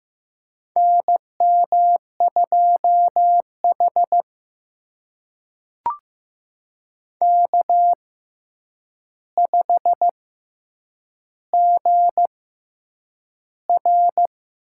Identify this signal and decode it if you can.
Morse code, heard as a keyed tone, about 15 wpm, spaced as for 8 wpm, 700 Hz. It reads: NM2H K5GR
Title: USA Call Signs - 4 Characters - 6x Character Spacing - 15wpm